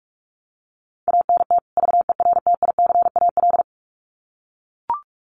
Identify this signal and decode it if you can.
Morse code, heard as a keyed tone, about 44 wpm, 700 Hz. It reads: ANT VERTICAL
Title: Common QSO Phrases 45wpm